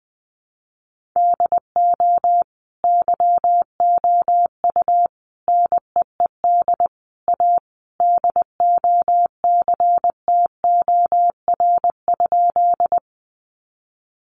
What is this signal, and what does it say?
Morse code, heard as a keyed tone, about 20 wpm, 700 Hz. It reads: DO YOU NEED A DOCTOR?
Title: Sentences from Top 1500 Words 20wpm